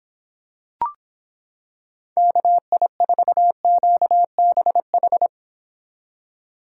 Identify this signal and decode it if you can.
Morse code, heard as a keyed tone, about 26 wpm, 700 Hz. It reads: KI4QBH